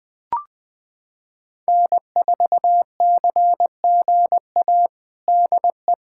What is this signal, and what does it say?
Morse code, heard as a keyed tone, about 20 wpm, 700 Hz. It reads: N4CGA DE
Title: USA Callsign DE Callsign - 5 Characters - 20wpm